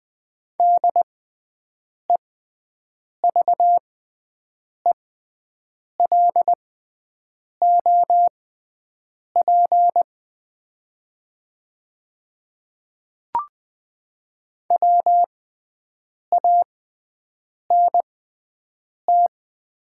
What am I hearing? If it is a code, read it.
Morse code, heard as a keyed tone, about 20 wpm, spaced as for 7 wpm, 700 Hz. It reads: DEVELOP WANT
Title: 6x Character Spacing - Top 500 Words - 20wpm